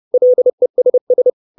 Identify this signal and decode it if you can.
Morse code, heard as a keyed tone, about 30 wpm, 500 Hz. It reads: LESS
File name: 632 Words 30WPM